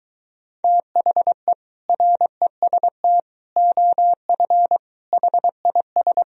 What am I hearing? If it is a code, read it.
Morse code, heard as a keyed tone, about 23 wpm, 700 Hz. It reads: THE REST OF HIS